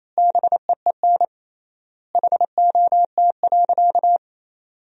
Morse code, heard as a keyed tone, about 28 wpm, 700 Hz. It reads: BEEN HOT.
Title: 3x Word-Spacing - Sentences from Top 500 Words - 28wpm